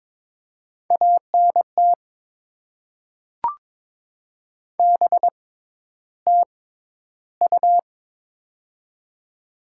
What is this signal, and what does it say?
Morse code, heard as a keyed tone, about 22 wpm, spaced as for 11 wpm, 700 Hz. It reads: ANT BTU
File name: Top 90 QSO Elements - 6x Character Spacing - 22wpm